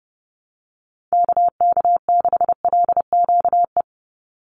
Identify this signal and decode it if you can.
Morse code, heard as a keyed tone, about 30 wpm, 700 Hz. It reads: KK6LQE
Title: USA Call Signs 30wpm